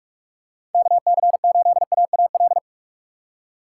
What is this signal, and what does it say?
Morse code, heard as a keyed tone, about 42 wpm, 700 Hz. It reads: KC8AAL
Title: USA Call Signs - 6 Characters - 3x Character Spacing - 45wpm